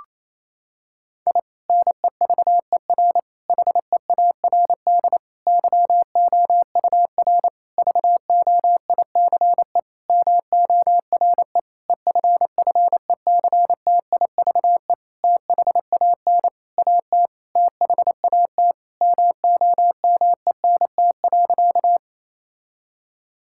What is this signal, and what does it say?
Morse code, heard as a keyed tone, about 28 wpm, 690 Hz. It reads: I NEVER HEARD YOUR VOICE MORE EFFECTIVE THAN AT THAT MOMENT.
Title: Sentences from Top 1000 Words - Review Part 6 - 28wpm